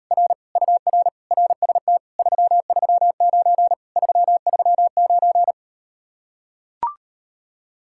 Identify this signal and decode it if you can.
Morse code, heard as a keyed tone, about 38 wpm, 700 Hz. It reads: R UR RST 339 339